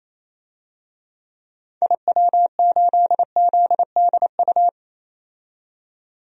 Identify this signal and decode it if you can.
Morse code, heard as a keyed tone, about 28 wpm, 700 Hz. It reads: IW8ZDU
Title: DXCC-100 Call Signs - 6 Characters - 28wpm